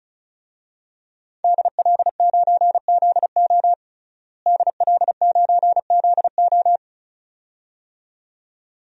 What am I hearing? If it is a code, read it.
Morse code, heard as a keyed tone, about 34 wpm, 700 Hz. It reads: DL9ZO DL9ZO
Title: DXCC-100 Call Signs - 5 Characters - Repeated 2x - 35wpm